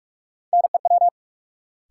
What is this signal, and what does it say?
Morse code, heard as a keyed tone, about 46 wpm, 700 Hz. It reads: NEW
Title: Sets of 2 Words - New Word Drill - 45wpm